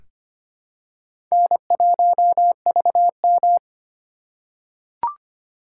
Morse code, heard as a keyed tone, about 25 wpm, 700 Hz. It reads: N1VM
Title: USA Call Signs - 3x Character Spacing - 25wpm